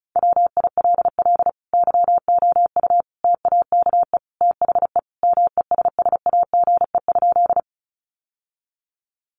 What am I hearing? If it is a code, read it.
Morse code, heard as a keyed tone, about 34 wpm, 700 Hz. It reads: WILL YOU TAKE THE MESSAGE?
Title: Sentences from Top 1100 Words 35wpm